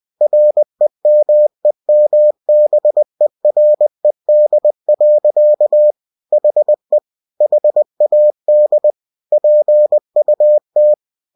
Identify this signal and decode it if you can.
Morse code, heard as a keyed tone, about 20 wpm, 600 Hz. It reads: REMEMBERED. HE HAD PUT
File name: cw-books / winnie-the-pooh / winnie-20wpm-page-0039